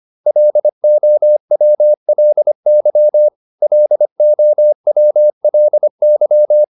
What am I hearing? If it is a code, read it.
Morse code, heard as a keyed tone, about 25 wpm, 600 Hz. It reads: LOWLY LOWLY